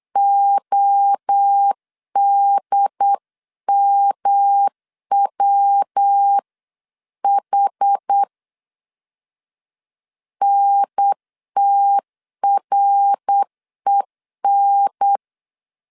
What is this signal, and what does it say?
Morse code, heard as a keyed tone, about 8 wpm, 800 Hz. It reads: ODMWH NTREN